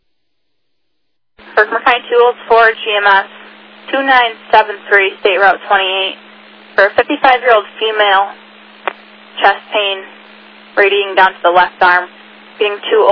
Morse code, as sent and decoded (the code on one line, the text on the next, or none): none